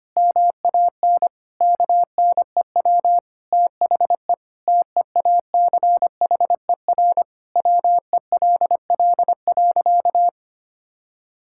Morse code, heard as a keyed tone, about 25 wpm, 700 Hz. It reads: MAN KNEW THE TEACHER WELL.
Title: Sentences from Top 1000 Words - Review Part 6 - 25wpm